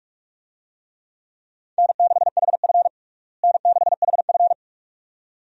Morse code, heard as a keyed tone, about 46 wpm, 700 Hz. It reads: N6HF N6HF
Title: USA Call Signs - 4 Characters - Repeated 2x - 45wpm